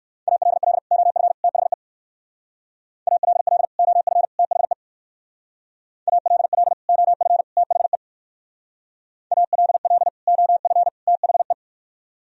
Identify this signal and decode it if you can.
Morse code, heard as a keyed tone, about 57 wpm, 700 Hz. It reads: ALL OF THE ALL OF THE ALL OF THE ALL OF THE